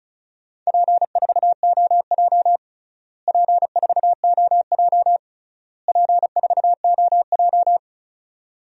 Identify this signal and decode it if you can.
Morse code, heard as a keyed tone, about 36 wpm, 700 Hz. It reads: P4OJ P4OJ P4OJ
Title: DXCC-100 Call Signs - 4 Characters - Repeated 3x - 35wpm